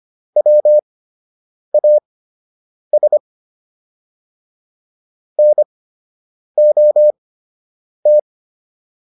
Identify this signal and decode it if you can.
Morse code, heard as a keyed tone, about 25 wpm, spaced as for 8 wpm, 600 Hz. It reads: WAS NOT